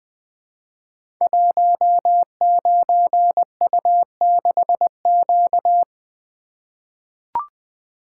Morse code, heard as a keyed tone, about 20 wpm, 700 Hz. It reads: 19U6Q
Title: Code Groups - 5 Letter:Numbers - 20wpm